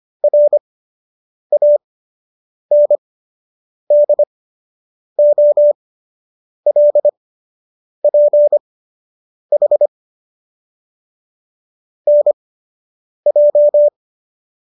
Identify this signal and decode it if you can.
Morse code, heard as a keyed tone, about 25 wpm, spaced as for 8 wpm, 590 Hz. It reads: RANDOLPH NJ